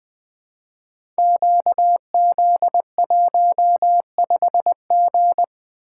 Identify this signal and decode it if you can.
Morse code, heard as a keyed tone, about 20 wpm, 700 Hz. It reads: QZ15G
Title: Code Groups - 5 Letter:Numbers - 20wpm